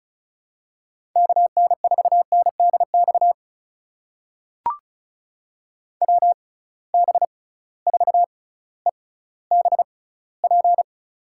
Morse code, heard as a keyed tone, about 36 wpm, spaced as for 18 wpm, 700 Hz. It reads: KN4NDX WB4EBP
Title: USA Call Signs - 6 Characters - 6x Character Spacing - 35wpm